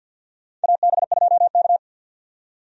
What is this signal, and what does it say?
Morse code, heard as a keyed tone, about 52 wpm, 700 Hz. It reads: AB2K
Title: USA Call Signs - 4 Characters - 50wpm